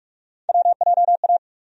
Morse code, heard as a keyed tone, about 41 wpm, 700 Hz. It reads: WJA
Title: USA Call Sign Postfixes - 3 Letters 45wpm